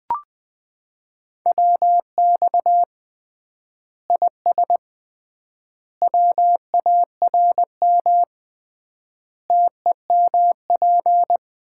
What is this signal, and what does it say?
Morse code, heard as a keyed tone, about 21 wpm, 700 Hz. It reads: WX IS WARM TEMP
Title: Common QSO Phrases - 3x Word-Spacing - 20wpm